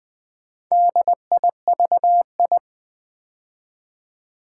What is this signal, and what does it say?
Morse code, heard as a keyed tone, about 20 wpm, 700 Hz. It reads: DIVI
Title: Build A Word - 2 Letters At A Time - 8-Letter Words - 20wpm